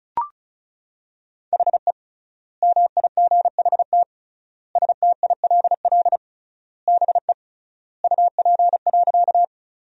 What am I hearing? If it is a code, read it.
Morse code, heard as a keyed tone, about 36 wpm, 700 Hz. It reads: HE MIGHT STILL BE UP.
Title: 3x Word-Spacing - Sentences from Top 300 Words - 35wpm